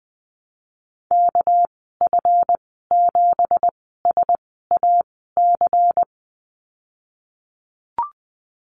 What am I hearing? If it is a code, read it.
Morse code, heard as a keyed tone, about 19 wpm, spaced as for 14 wpm, 700 Hz. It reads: KF7SAC
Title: 207 Lesson 33g - Letter j - Call Signs 20wpm